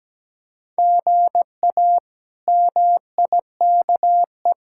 Morse code, heard as a keyed tone, about 17 wpm, 700 Hz. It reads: GA MIKE
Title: K1USN SST QSO Practice - 17wpm